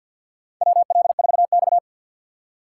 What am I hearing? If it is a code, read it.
Morse code, heard as a keyed tone, about 47 wpm, 700 Hz. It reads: WL4X
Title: USA Call Signs - 4 Characters - 3x Character Spacing - 50wpm